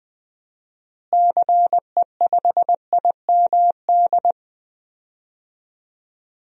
Morse code, heard as a keyed tone, about 20 wpm, 700 Hz. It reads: CE5IMD
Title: DXCC-100 Call Signs - 6 Characters - Repeated 2x - 20wpm